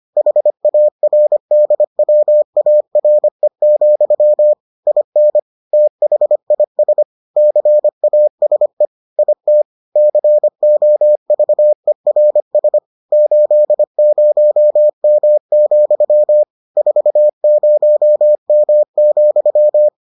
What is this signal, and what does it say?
Morse code, heard as a keyed tone, about 25 wpm, 600 Hz. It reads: HARDWARE, IN THIS CASE IT COVERS 80M, 40M,